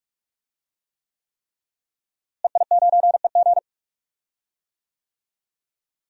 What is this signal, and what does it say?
Morse code, heard as a keyed tone, about 47 wpm, 700 Hz. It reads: EI9EG
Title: DXCC-100 Call Signs - 5 Characters - Repeated 3x - 45wpm